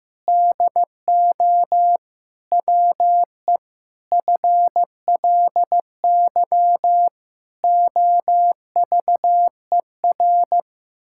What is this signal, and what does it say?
Morse code, heard as a keyed tone, about 16 wpm, 700 Hz. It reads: DO WE FLY OVER
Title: Pangram Challenge v2 - 15wpm